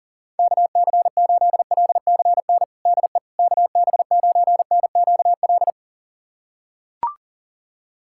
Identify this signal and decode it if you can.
Morse code, heard as a keyed tone, about 40 wpm, 700 Hz. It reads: KC8LCN DE KB9NQL
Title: USA Callsign DE Callsign - 6 Characters - 40wpm